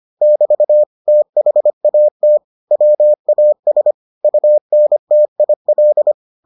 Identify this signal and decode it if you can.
Morse code, heard as a keyed tone, about 25 wpm, 600 Hz. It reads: <BT> THAT WAS UNTIL